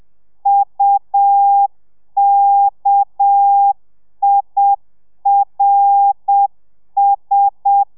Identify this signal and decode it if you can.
Morse code, heard as a keyed tone, about 7 wpm, 800 Hz. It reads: UKIRS